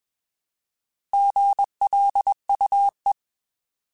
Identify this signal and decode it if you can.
Morse code, heard as a keyed tone, about 20 wpm, 800 Hz. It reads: GLUE
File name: Alphabetical / Alphabetical mp3s / Gs @20